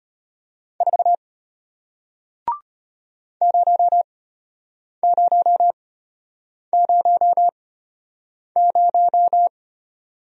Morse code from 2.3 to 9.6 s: 0000

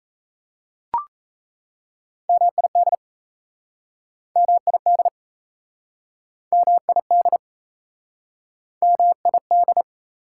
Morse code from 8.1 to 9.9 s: MID

